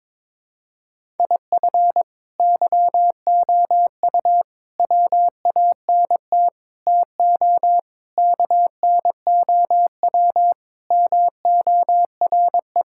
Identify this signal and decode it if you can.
Morse code, heard as a keyed tone, about 22 wpm, 700 Hz. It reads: IF YOU WANT TO KNOW MORE
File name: practice / 22wpm / Sets of 8 Words - 22wpm